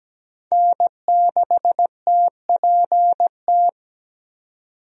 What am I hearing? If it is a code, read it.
Morse code, heard as a keyed tone, about 17 wpm, 700 Hz. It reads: N6TPT
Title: USA Call Signs - 5 Characters - 17wpm